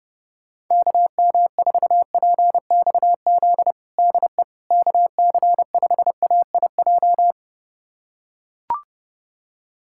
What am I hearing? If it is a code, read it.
Morse code, heard as a keyed tone, about 30 wpm, 700 Hz. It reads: KM4PXZ DE KC5AIJ
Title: USA Callsign DE Callsign - 6 Characters - 30wpm